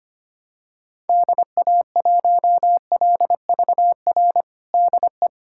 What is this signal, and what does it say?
Morse code, heard as a keyed tone, about 25 wpm, 700 Hz. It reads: DA1LVR DE
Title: USA Callsign DE Callsign - 6 Characters - 25wpm